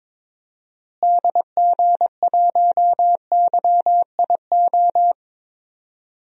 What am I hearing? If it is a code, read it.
Morse code, heard as a keyed tone, about 22 wpm, 700 Hz. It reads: DG1YIO
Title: DXCC-100 Call Signs - 6 Characters - Repeated 2x - 22wpm